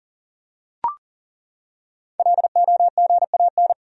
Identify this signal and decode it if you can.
Morse code, heard as a keyed tone, about 40 wpm, 700 Hz. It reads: LOGAN
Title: USA Common Names - 40wpm